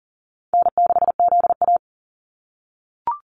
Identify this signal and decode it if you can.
Morse code, heard as a keyed tone, about 41 wpm, 700 Hz. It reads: N6ZA
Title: NA SOTA Activator Callsigns 40wpm